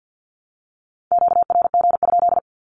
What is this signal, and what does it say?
Morse code, heard as a keyed tone, about 48 wpm, 700 Hz. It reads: QRG?
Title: Q-Codes 50wpm